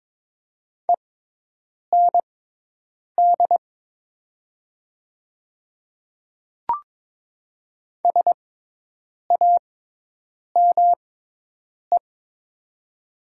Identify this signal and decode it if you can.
Morse code, heard as a keyed tone, about 21 wpm, spaced as for 8 wpm, 700 Hz. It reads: END SAME
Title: 6x Character Spacing - Top 500 Words - 22wpm